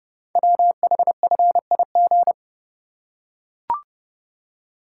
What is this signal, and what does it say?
Morse code, heard as a keyed tone, about 30 wpm, 700 Hz. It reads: WHFIG
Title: Code Groups - 5 Letters - 30wpm